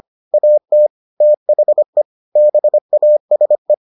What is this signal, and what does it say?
Morse code, heard as a keyed tone, about 25 wpm, 600 Hz. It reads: AT THE BASE